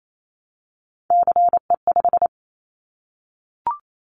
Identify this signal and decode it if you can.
Morse code, heard as a keyed tone, about 28 wpm, 700 Hz. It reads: CE5
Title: Callsign Prefixes for Top 100 DXCC - 28wpm